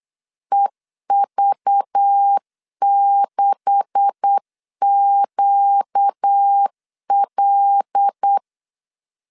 Morse code, heard as a keyed tone, about 8 wpm, 800 Hz. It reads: EV6QL
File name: LEZIONI / 10 SIGNS / 10C DOT 8wpm 3649